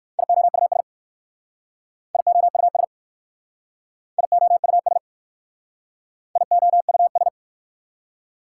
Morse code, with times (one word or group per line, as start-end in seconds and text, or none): 0.2-0.8 IOUS
2.1-2.8 IOUS
4.2-5.0 IOUS
6.4-7.3 IOUS